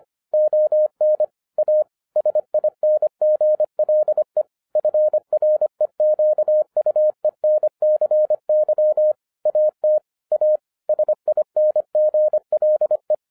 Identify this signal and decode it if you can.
Morse code, heard as a keyed tone, about 25 wpm, 610 Hz. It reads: ON A SINGLE FREQUENCY AT A SINGLE
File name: 2012-01-mp3 / n14-FUN WITH NUMBERS-25wpm